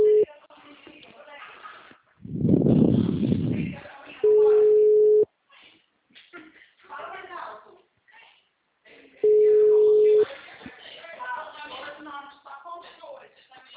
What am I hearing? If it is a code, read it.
Ringback tone (Europe, 425 Hz), with speech.